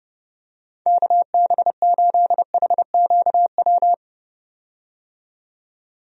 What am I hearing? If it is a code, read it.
Morse code, heard as a keyed tone, about 30 wpm, 700 Hz. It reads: KB8HQW
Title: USA Call Signs - 6 Characters - 30wpm